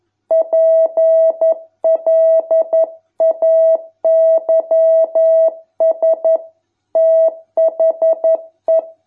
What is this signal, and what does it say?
Morse code, heard as a keyed tone, about 11 wpm, 650 Hz. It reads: PLAYS THE